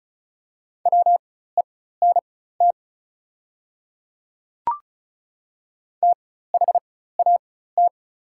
Morse code, heard as a keyed tone, about 36 wpm, spaced as for 16 wpm, 700 Hz. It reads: WENT THAT
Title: 4x Character Spacing - Top 200 Words - 35wpm